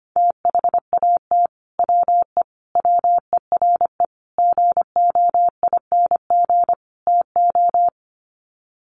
Morse code, heard as a keyed tone, about 25 wpm, 700 Hz. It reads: THAT WE WERE GOING TO
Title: Sets of 5 Words Encore2 25wpm